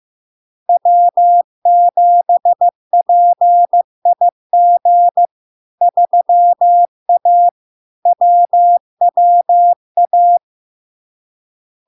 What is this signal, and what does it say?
Morse code, heard as a keyed tone, about 15 wpm, 700 Hz. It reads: W7PIG 3A WWA
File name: LARK-FD-2022-Log-in-CW-15wpm-700-Hz